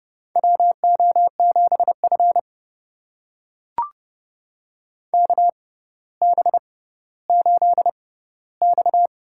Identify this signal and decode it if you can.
Morse code, heard as a keyed tone, about 30 wpm, spaced as for 15 wpm, 700 Hz. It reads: WO7F KB8X
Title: USA Call Signs - 4 Characters - 6x Character Spacing - 30wpm